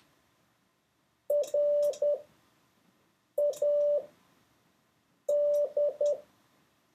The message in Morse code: RAD